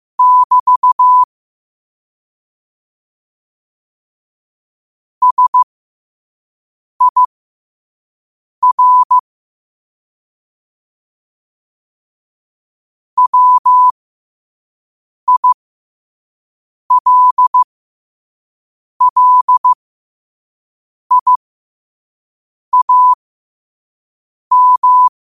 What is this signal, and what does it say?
Morse code, heard as a keyed tone, about 15 wpm, spaced as for 5 wpm, 1000 Hz. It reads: <BT> SIR WILLIAM
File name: qotdfiles / 5wpm